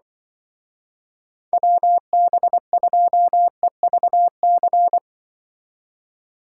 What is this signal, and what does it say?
Morse code, heard as a keyed tone, about 24 wpm, 700 Hz. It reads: WB2EVC
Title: Speed Racing - USA 2x3 Call Signs 20wpm